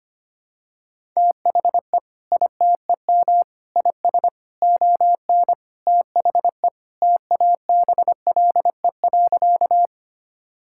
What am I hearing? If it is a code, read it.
Morse code, heard as a keyed tone, about 25 wpm, 700 Hz. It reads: THE ITEM IS ON THE TABLE.
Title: Sentences from Top 1000 Words - Review Part 6 - 25wpm